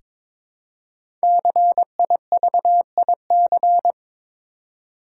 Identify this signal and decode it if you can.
Morse code, heard as a keyed tone, about 22 wpm, 700 Hz. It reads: CIVIC
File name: 5 Letter Words - First 3 Letters Given - 22wpm